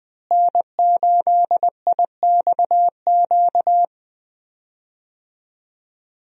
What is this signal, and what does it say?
Morse code, heard as a keyed tone, about 20 wpm, 700 Hz. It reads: N8IXQ
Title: USA Call Signs - 5 Characters - Repeated 2x - 20wpm